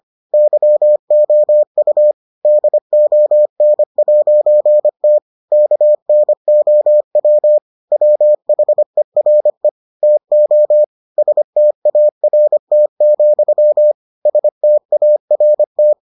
Morse code, heard as a keyed tone, about 25 wpm, 600 Hz. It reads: YOU DON'T KNOW WHERE TO START, START